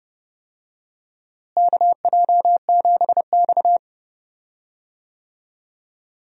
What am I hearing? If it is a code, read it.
Morse code, heard as a keyed tone, about 32 wpm, 700 Hz. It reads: KJ7X
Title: USA Call Signs - 4 Characters - Repeated 3x - 30wpm